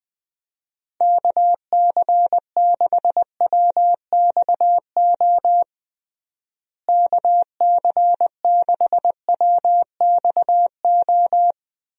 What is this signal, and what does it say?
Morse code, heard as a keyed tone, about 20 wpm, 700 Hz. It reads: KC6WXO KC6WXO
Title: USA Call Signs - 6 Characters - Repeated 2x - 20wpm